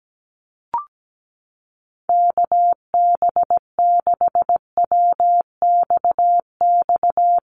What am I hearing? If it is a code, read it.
Morse code, heard as a keyed tone, about 17 wpm, 700 Hz. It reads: KB6WXX
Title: USA Call Signs Repeated 2x 17wpm